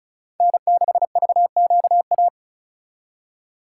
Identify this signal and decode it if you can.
Morse code, heard as a keyed tone, about 44 wpm, 700 Hz. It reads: N6VQA
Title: Speed Racing - USA 1x3 Call Signs 35wpm